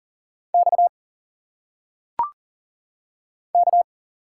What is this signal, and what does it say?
Morse code, heard as a keyed tone, about 39 wpm, 700 Hz. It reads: X K
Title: Single Letters - Phonetic Pronunciation 40wpm